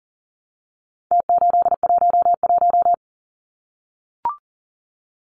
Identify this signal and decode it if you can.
Morse code, heard as a keyed tone, about 40 wpm, 700 Hz. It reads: T811
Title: Numbers - Contest Style - 4-Digit 40wpm